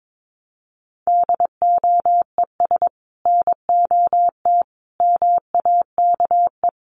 Morse code, heard as a keyed tone, about 22 wpm, 700 Hz. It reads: DOES NOT MAKE
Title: Sets of 5 Words Encore2 22wpm